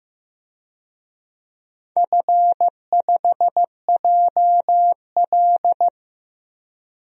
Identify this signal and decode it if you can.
Morse code, heard as a keyed tone, about 15 wpm, 700 Hz. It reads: F5JL
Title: DXCC-100 Call Signs - 4 Characters - 15wpm